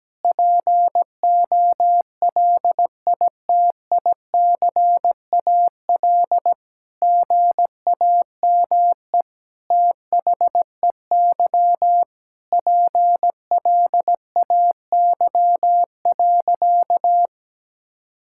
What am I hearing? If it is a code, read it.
Morse code, heard as a keyed tone, about 17 wpm, 700 Hz. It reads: POLITICAL GAME THEY PLAY.
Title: Sentences from Top 1000 Words - Review Part 5 - 17wpm